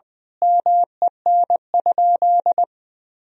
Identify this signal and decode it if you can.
Morse code, heard as a keyed tone, about 21 wpm, 700 Hz. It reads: MEN?
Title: Sentences from Top 200 Words and Abbreviations 20wpm